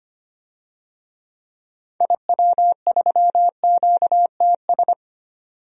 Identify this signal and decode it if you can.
Morse code, heard as a keyed tone, about 25 wpm, 700 Hz. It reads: IW3QTS
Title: DXCC-100 Call Signs - 6 Characters - 25wpm